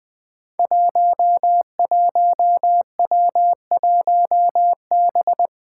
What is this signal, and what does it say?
Morse code, heard as a keyed tone, about 20 wpm, 700 Hz. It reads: 11W1B
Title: Five Challenging Letter-Number Combinations 20wpm